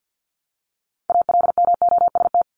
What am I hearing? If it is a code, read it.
Morse code, heard as a keyed tone, about 50 wpm, 700 Hz. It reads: ALMOST
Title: Top 501-750 Words 50wpm